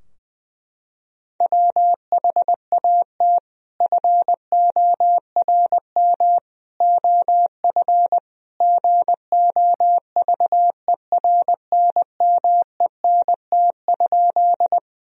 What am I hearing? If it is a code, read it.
Morse code, heard as a keyed tone, about 21 wpm, 700 Hz. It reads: WHAT FORM OF GOVERNMENT?
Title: Sentences from Top 1000 Words - Review Part 3 - 20wpm